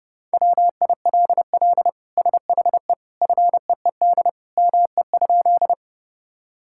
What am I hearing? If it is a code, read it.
Morse code, heard as a keyed tone, about 30 wpm, 700 Hz. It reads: WILL SHE FEED ME?
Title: Sentences from Top 1600 Words 30wpm